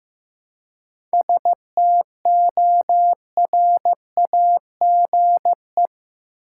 Morse code, heard as a keyed tone, about 15 wpm, 700 Hz. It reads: STORAGE
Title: Seven Letter Words - 15wpm